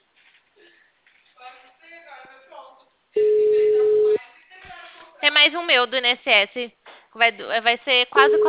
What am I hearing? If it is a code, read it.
Ringback tone (Europe, 425 Hz), with speech.
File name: out- -2015- - - .51407